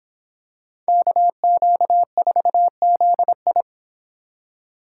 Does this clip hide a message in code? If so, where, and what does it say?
Yes: Morse code from 0.9 to 3.6 s, heard as a keyed tone, about 26 wpm, 700 Hz. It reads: KQ4ZI